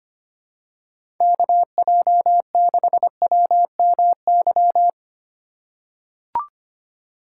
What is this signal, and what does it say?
Morse code, heard as a keyed tone, about 25 wpm, 700 Hz. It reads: KJ6WMY
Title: USA Call Signs - 6 Characters - 3x Character Spacing - 25wpm